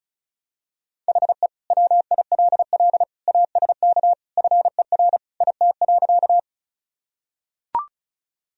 Morse code, heard as a keyed tone, about 36 wpm, 700 Hz. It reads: HE WILL ASK FER IT.
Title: Sentences from Top 200 Words and Abbreviations 35wpm